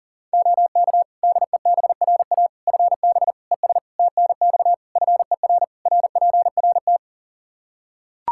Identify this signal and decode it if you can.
Morse code, heard as a keyed tone, about 40 wpm, 700 Hz. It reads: OK DEBRA FB ES TNX FER RPRT
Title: Practice QSO - Exchange 2 - Info - 40wpm